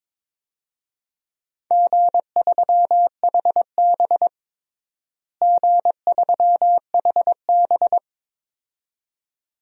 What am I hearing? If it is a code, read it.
Morse code, heard as a keyed tone, about 22 wpm, 700 Hz. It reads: G3HB G3HB
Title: DXCC-100 Call Signs - 4 Characters - Repeated 2x - 22wpm